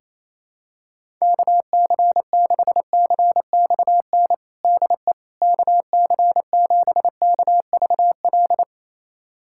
Morse code, heard as a keyed tone, about 28 wpm, 700 Hz. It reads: KC6CXN DE KC7KVL
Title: USA Callsign DE Callsign - 6 Characters - 28wpm